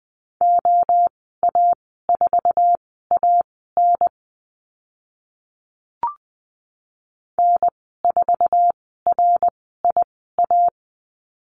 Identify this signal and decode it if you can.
Morse code, heard as a keyed tone, about 20 wpm, spaced as for 14 wpm, 700 Hz. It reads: OA4AN N4RIA
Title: 076 Lesson 13g - Letter L - Call Signs 20wpm